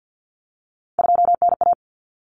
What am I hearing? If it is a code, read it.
Morse code, heard as a keyed tone, about 50 wpm, 700 Hz. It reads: 2NA
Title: Common Cut Numbers - 3-Digit 50wpm